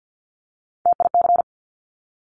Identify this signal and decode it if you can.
Morse code, heard as a keyed tone, about 50 wpm, 700 Hz. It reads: TIC
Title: Top Three Letter Combinations in Words 50wpm